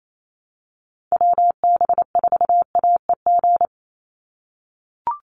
Morse code, heard as a keyed tone, about 28 wpm, 700 Hz. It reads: WB4AEG
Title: Call Signs - 28wpm